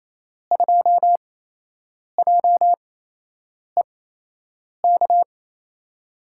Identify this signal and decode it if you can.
Morse code, heard as a keyed tone, about 28 wpm, spaced as for 8 wpm, 700 Hz. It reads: 2JEK
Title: USA Call Signs - 5 Characters - 8x Character Spacing - 28wpm